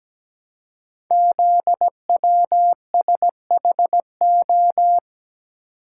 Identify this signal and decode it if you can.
Morse code, heard as a keyed tone, about 17 wpm, 700 Hz. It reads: ZWSHO
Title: Code Groups - 5 Letters - 17wpm